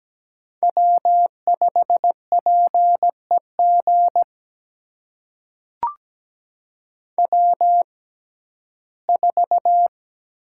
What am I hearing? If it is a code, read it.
Morse code, heard as a keyed tone, about 17 wpm, spaced as for 12 wpm, 700 Hz. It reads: W5PEG W4
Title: USA Call Signs - 5 Characters - 6x Character Spacing - 17wpm